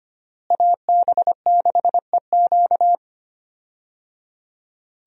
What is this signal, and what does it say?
Morse code, heard as a keyed tone, about 25 wpm, 700 Hz. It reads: AB6EQ
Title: USA Call Signs - 5 Characters - Repeated 3x - 25wpm